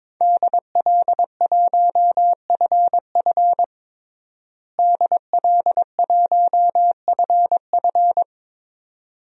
Morse code, heard as a keyed tone, about 22 wpm, 700 Hz. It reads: DL1FF DL1FF
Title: DXCC-100 Call Signs - 5 Characters - Repeated 2x - 22wpm